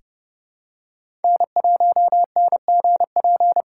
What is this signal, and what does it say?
Morse code, heard as a keyed tone, about 31 wpm, 700 Hz. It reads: N1NGP
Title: USA Call Signs - 5 Characters - 3x Character Spacing - 30wpm